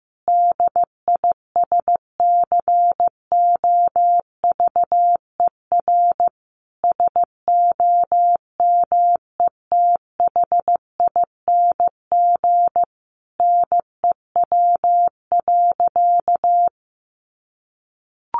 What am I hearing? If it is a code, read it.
Morse code, heard as a keyed tone, about 15 wpm, 700 Hz. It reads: DISCOVER SOMETHING NEW.
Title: Sentences from Top 1100 Words 15wpm